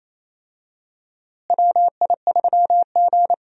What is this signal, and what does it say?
Morse code, heard as a keyed tone, about 28 wpm, 700 Hz. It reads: WI3G
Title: USA Call Signs - 4 Characters - Repeated 3x - 28wpm